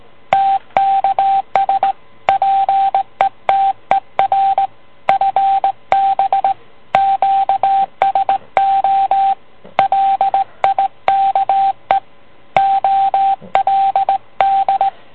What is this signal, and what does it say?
Morse code, heard as a keyed tone, about 18 wpm, 780 Hz, with radio noise. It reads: TKS PETER FB QSO LIKE OLD